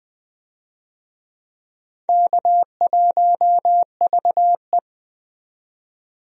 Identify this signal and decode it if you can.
Morse code, heard as a keyed tone, about 20 wpm, 700 Hz. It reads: K1VE
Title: USA Call Signs - 4 Characters - 20wpm